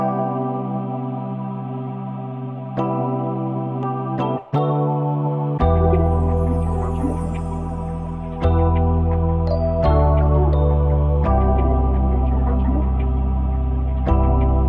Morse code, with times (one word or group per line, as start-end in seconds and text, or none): none